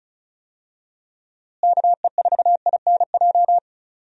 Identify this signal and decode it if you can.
Morse code, heard as a keyed tone, about 34 wpm, 700 Hz. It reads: KE4INJ
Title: USA Call Signs - 6 Characters - Repeated 3x - 35wpm